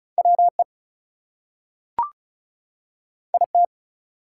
Morse code, heard as a keyed tone, about 36 wpm, 700 Hz. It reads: WE IT
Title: Two Letter Words - 35wpm